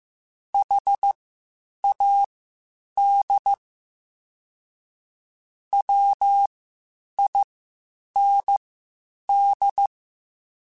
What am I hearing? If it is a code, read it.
Morse code, heard as a keyed tone, about 15 wpm, spaced as for 8 wpm, 800 Hz. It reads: HAD WIND
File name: Spaced Characters / Spaced Characters-mp3s / Weather 5 5 sp @13